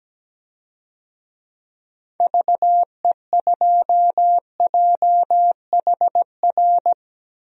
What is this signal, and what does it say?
Morse code, heard as a keyed tone, about 17 wpm, 700 Hz. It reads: VE2JHR
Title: DXCC-100 Call Signs - 6 Characters - 17wpm